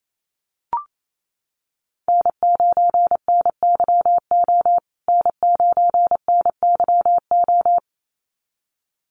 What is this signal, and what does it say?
Morse code, heard as a keyed tone, about 28 wpm, 700 Hz. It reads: N9NYO N9NYO
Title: USA Call Signs Repeated 2x 28wpm